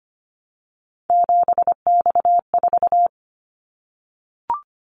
Morse code, heard as a keyed tone, about 25 wpm, 700 Hz. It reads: 7X4
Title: Callsign Prefixes for Top 100 DXCC - 25wpm